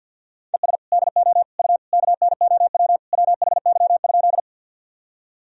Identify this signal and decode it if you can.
Morse code, heard as a keyed tone, about 50 wpm, 700 Hz. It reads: ES DO U KNOW WHY?
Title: Sentences from Top 200 Words and Abbreviations 50wpm